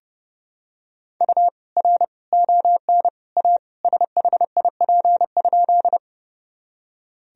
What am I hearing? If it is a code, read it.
Morse code, heard as a keyed tone, about 30 wpm, 700 Hz. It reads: U R ON A SHIP?
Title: Sentences from Top 400 Words and Abbreviations 30wpm